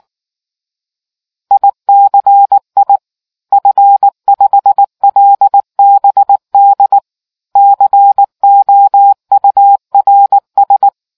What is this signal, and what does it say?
Morse code, heard as a keyed tone, about 19 wpm, 800 Hz. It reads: ICI F5LBD COURS